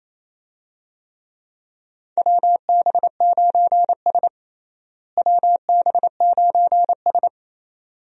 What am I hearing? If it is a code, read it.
Morse code, heard as a keyed tone, about 28 wpm, 700 Hz. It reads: WB9S WB9S
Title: USA Call Signs - 4 Characters - Repeated 2x - 28wpm